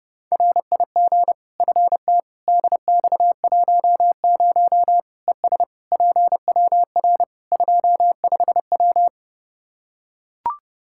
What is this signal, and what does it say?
Morse code, heard as a keyed tone, about 30 wpm, 700 Hz. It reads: RIG FT DX10 ES PWR 25W
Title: Practice QSO - Exchange 2 - Info - 30wpm